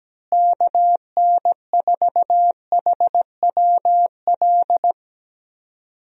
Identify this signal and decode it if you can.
Morse code, heard as a keyed tone, about 17 wpm, 700 Hz. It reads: KN4HWL
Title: USA Call Signs - 6 Characters - Repeated 3x - 17wpm